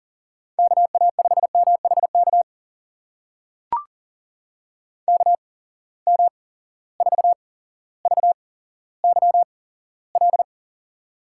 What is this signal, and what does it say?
Morse code, heard as a keyed tone, about 40 wpm, spaced as for 17 wpm, 700 Hz. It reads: KA5MHK KM4VYL
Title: USA Call Signs - 6 Characters - 8x Character Spacing - 40wpm